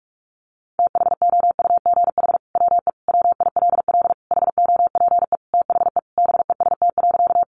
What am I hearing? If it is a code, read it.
Morse code, heard as a keyed tone, about 44 wpm, 700 Hz. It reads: THOUGH WE WILL HOPE THE BEST.
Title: Sentences from Top 1000 Words 45wpm